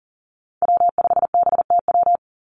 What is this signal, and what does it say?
Morse code, heard as a keyed tone, about 42 wpm, 700 Hz. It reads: W5BTW
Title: USA Call Signs 40wpm